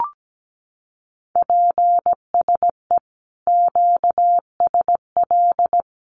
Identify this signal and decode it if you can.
Morse code, heard as a keyed tone, about 17 wpm, 700 Hz. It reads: PSE QSL
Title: Common QSO Phrases 17wpm